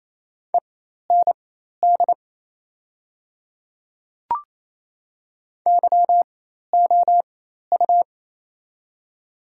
Morse code, heard as a keyed tone, about 28 wpm, spaced as for 13 wpm, 690 Hz. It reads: END YOU
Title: 4x Character Spacing - Top 300 Words - 28wpm